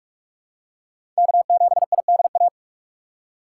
Morse code, heard as a keyed tone, about 45 wpm, 700 Hz. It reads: K7IDA
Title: USA Call Signs - 5 Characters - 3x Character Spacing - 45wpm